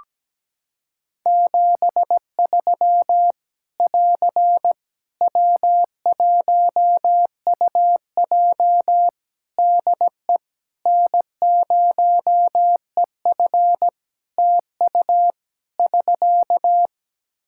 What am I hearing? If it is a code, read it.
Morse code, heard as a keyed tone, about 17 wpm, 700 Hz. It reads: 73 <AR> W1UJ DE N0EF TU <SK>
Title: Practice QSO - Exchange 4 - Ending - 17wpm